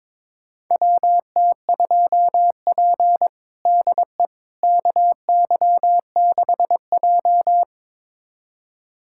Morse code, heard as a keyed tone, about 22 wpm, 700 Hz. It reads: WT2P DE KY6J
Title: USA Callsign DE Callsign - 4 Characters - 22wpm